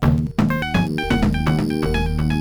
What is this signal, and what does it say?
There is no code.